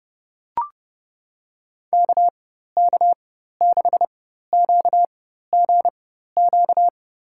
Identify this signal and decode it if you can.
Morse code, heard as a keyed tone, about 30 wpm, spaced as for 14 wpm, 700 Hz. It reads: KK6QGQ
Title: USA Call Signs - 6 Characters - 4x Character Spacing - 30wpm